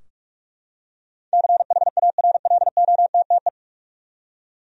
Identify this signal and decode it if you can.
Morse code, heard as a keyed tone, about 45 wpm, 700 Hz. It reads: CHARLOTTE
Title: USA Common Names - 45wpm